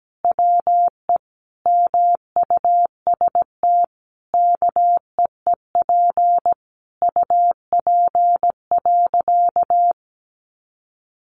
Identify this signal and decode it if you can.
Morse code, heard as a keyed tone, about 17 wpm, 700 Hz. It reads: WE MUST KEEP UP.
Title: Sentences from Top 700 Words 17wpm